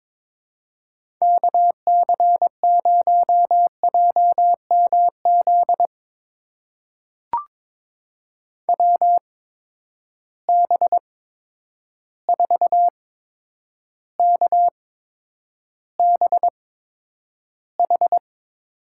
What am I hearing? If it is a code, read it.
Morse code, heard as a keyed tone, about 21 wpm, spaced as for 9 wpm, 700 Hz. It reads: KC0JMZ WB4KBH
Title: USA Call Signs - 6 Characters - 8x Character Spacing - 22wpm